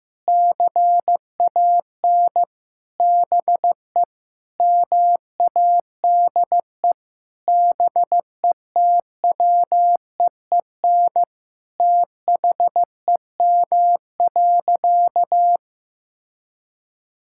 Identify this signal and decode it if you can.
Morse code, heard as a keyed tone, about 15 wpm, 700 Hz. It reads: CAN BE MADE BETWEEN THEM.
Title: Sentences from Top 1000 Words - Review Part 6 - 15wpm